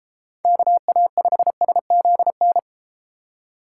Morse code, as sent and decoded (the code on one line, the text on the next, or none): -.- .- ..... ... --.. -.
KA5SZN